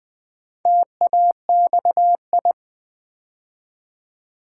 Morse code, heard as a keyed tone, about 20 wpm, 700 Hz. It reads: TAXI
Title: Build A Word - 2 Letters At A Time - 4-Letter Words - 20wpm